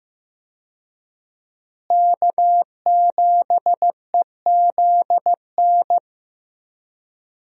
Morse code, heard as a keyed tone, about 15 wpm, 700 Hz. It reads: K7EZN
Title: USA Call Signs - 5 Characters - Repeated 2x - 15wpm